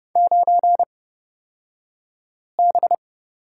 Morse code, heard as a keyed tone, about 30 wpm, spaced as for 5 wpm, 690 Hz. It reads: 9B
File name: Lesson 36d - BT - Two Characters 30wpm